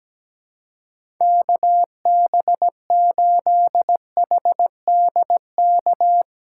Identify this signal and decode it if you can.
Morse code, heard as a keyed tone, about 17 wpm, 700 Hz. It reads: KB8HDK
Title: USA Call Signs - 6x Character Spacing - 17wpm